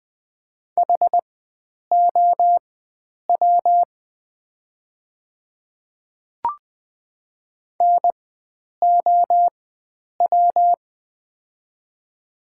Morse code, heard as a keyed tone, about 20 wpm, spaced as for 9 wpm, 690 Hz. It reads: HOW NOW
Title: 4x Character Spacing - Top 100 Words - 20wpm